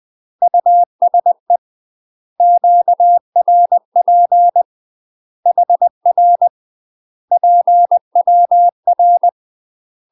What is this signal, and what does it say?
Morse code, heard as a keyed tone, about 20 wpm, 700 Hz. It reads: USE QRP HR PWR
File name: UK MASTER QSOs TRAINING / 20WPM / UK MASTER QSOs 0039